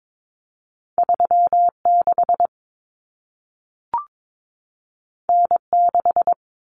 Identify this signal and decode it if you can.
Morse code, heard as a keyed tone, about 22 wpm, 700 Hz. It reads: 36 N6
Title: Common Cut Numbers - 2-Digit 22wpm